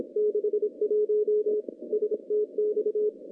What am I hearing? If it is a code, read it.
Morse code, heard as a keyed tone, about 25 wpm, 410 Hz. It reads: 61 STX
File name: contests / CW-SS-k5tr- -0200